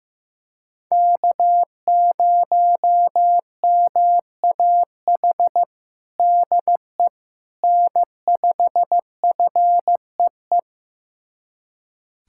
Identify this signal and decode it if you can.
Morse code, heard as a keyed tone, about 15 wpm, 700 Hz. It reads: K0MAH DE N5FEE
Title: USA Callsign DE Callsign 15wpm